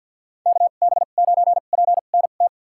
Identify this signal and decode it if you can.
Morse code, heard as a keyed tone, about 50 wpm, spaced as for 36 wpm, 700 Hz. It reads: KB9PNT
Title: USA Call Signs - 2x Character Spacing - 50wpm